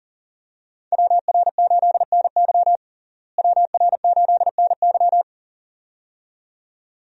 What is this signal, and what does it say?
Morse code, heard as a keyed tone, about 37 wpm, 700 Hz. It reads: WR8NY WR8NY
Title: USA Call Signs - 5 Characters - Repeated 2x - 40wpm